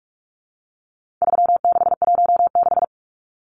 Morse code, heard as a keyed tone, about 44 wpm, 700 Hz. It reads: 3616
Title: Numbers - 4-Digit 45wpm